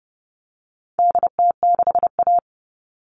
Morse code, heard as a keyed tone, about 30 wpm, 700 Hz. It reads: DT6A
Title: All Cut Numbers - 4-Digit 30wpm